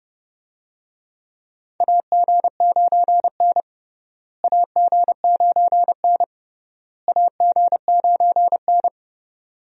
Morse code, heard as a keyed tone, about 32 wpm, 700 Hz. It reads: AG9N AG9N AG9N
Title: USA Call Signs - 4 Characters - Repeated 3x - 30wpm